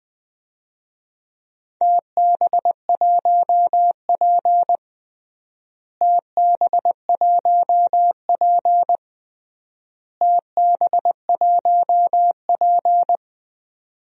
Morse code, heard as a keyed tone, about 21 wpm, 700 Hz. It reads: TB1P TB1P TB1P
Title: DXCC-100 Call Signs - 4 Characters - Repeated 3x - 20wpm